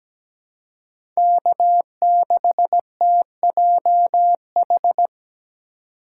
Morse code, heard as a keyed tone, about 17 wpm, 700 Hz. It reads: K6TJH
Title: USA Call Signs - 5 Characters - 4x Character Spacing - 17wpm